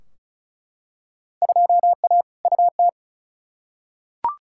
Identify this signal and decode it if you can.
Morse code, heard as a keyed tone, about 34 wpm, 700 Hz. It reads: 2A UT
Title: Field Day Exchanges - 35wpm